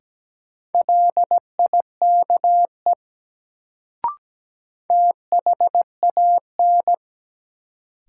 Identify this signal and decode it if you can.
Morse code, heard as a keyed tone, about 17 wpm, 700 Hz. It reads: LIKE THAN
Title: Sentences from Top 100 Words - New Word Drill - 17wpm